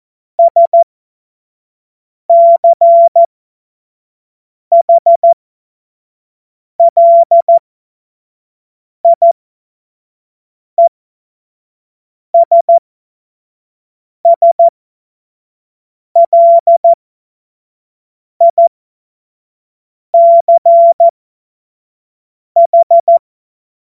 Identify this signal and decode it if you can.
Morse code, heard as a keyed tone, about 14 wpm, spaced as for 5 wpm, 680 Hz. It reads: SCHLIESSLICH